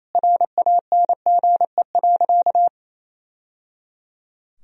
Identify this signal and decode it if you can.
Morse code, heard as a keyed tone, about 28 wpm, 700 Hz. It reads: RANGE.
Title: Sentences from Top 1000 Words - Review Part 4 - 28wpm